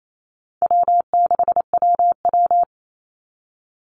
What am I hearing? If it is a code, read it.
Morse code, heard as a keyed tone, about 28 wpm, 700 Hz. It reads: W6WW
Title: NA SOTA Chaser Callsigns 28wpm